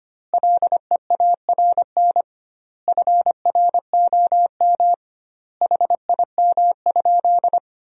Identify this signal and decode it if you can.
Morse code, heard as a keyed tone, about 25 wpm, 700 Hz. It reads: LEARN FROM HIM?
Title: 2x Word-Spacing - Sentences from Top 300 Words - 25wpm